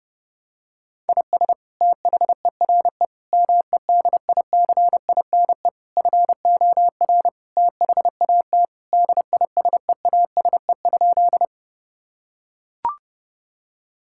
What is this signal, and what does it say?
Morse code, heard as a keyed tone, about 30 wpm, 700 Hz. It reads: IS THERE MEDICINE FOR THAT DISEASE?
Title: Sentences from Top 1000 Words - Review Part 6 - 30wpm